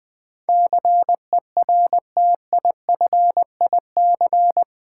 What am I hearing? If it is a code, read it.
Morse code, heard as a keyed tone, about 20 wpm, 700 Hz. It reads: CERTIFIC
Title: Build A Word - 4 Letters At A Time - 11-Letter Words - 20wpm